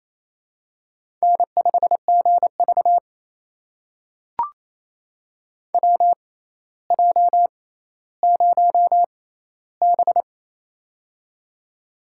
Morse code, heard as a keyed tone, about 28 wpm, spaced as for 14 wpm, 700 Hz. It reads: N5GV WJ0B
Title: USA Call Signs - 4 Characters - 6x Character Spacing - 28wpm